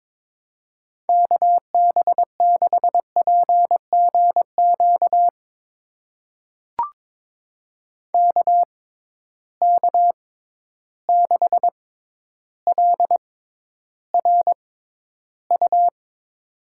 Morse code, heard as a keyed tone, about 22 wpm, spaced as for 11 wpm, 700 Hz. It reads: KB6PGQ KK6LRU
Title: USA Call Signs - 6 Characters - 6x Character Spacing - 22wpm